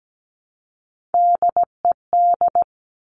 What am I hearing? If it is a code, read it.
Morse code, heard as a keyed tone, about 17 wpm, 700 Hz. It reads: DED
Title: Top Three Letter Combinations in Words 17wpm